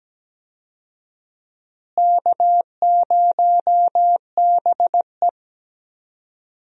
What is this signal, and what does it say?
Morse code, heard as a keyed tone, about 17 wpm, 700 Hz. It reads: K0BE
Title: USA Call Signs - 4 Characters - Repeated 3x - 17wpm